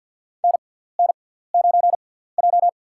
Morse code, heard as a keyed tone, about 53 wpm, spaced as for 17 wpm, 690 Hz. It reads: NN9J
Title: USA Call Signs - 6x Character Spacing - 50wpm